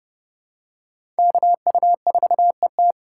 Morse code, heard as a keyed tone, about 32 wpm, 700 Hz. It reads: KU4ET
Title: USA Call Signs - 5 Characters - 3x Character Spacing - 30wpm